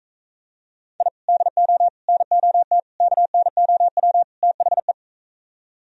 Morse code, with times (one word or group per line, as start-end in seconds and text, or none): 1.0-1.1 I
1.3-1.9 DO
2.1-2.8 NOT
3.0-4.2 KNOW
4.4-4.9 THE